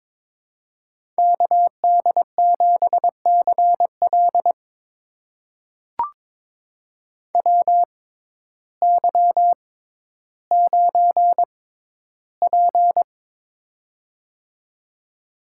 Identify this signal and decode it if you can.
Morse code, heard as a keyed tone, about 21 wpm, spaced as for 12 wpm, 700 Hz. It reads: KD7CL WY9P
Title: USA Call Signs - 6x Character Spacing - 22wpm